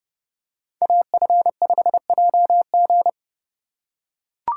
Morse code, heard as a keyed tone, about 30 wpm, 700 Hz. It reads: AF5JG